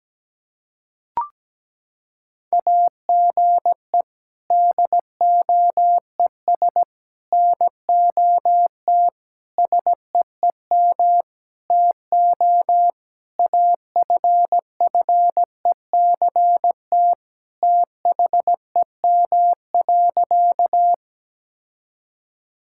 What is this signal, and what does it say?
Morse code, heard as a keyed tone, about 17 wpm, 700 Hz. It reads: AGE DOES NOT SEEM TO AFFECT THEM.
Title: Sentences from Top 1000 Words - Review Part 6 - 17wpm